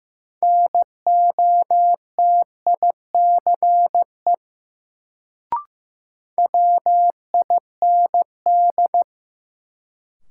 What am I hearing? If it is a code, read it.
Morse code, heard as a keyed tone, about 15 wpm, 700 Hz. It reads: NOTICE WIND
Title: Sentences from Top 400 Words - New Word Drill - 15wpm